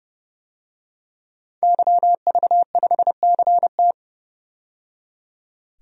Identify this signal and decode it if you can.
Morse code, heard as a keyed tone, about 31 wpm, 700 Hz. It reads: YV5CT
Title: DXCC-100 Call Signs - 5 Characters - 30wpm